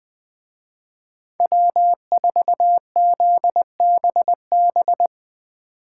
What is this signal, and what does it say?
Morse code, heard as a keyed tone, about 21 wpm, 700 Hz. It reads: W4ZBB